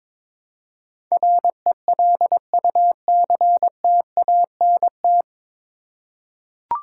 Morse code, heard as a keyed tone, about 22 wpm, 700 Hz. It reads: RELUCTANT
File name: 9 Letter Words - First 3 Letters Given - 22wpm